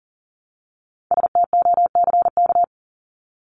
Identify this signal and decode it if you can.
Morse code, heard as a keyed tone, about 40 wpm, 700 Hz. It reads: STOCK
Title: Top 501-750 Words 40wpm